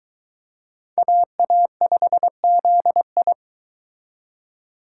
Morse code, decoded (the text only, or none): AA5ZI